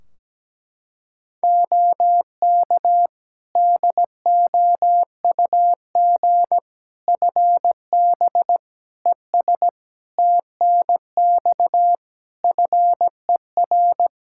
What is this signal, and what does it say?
Morse code, heard as a keyed tone, about 17 wpm, 700 Hz. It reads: OK DOUG FB ES TNX FER
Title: Practice QSO - Exchange 2 - Info - 17wpm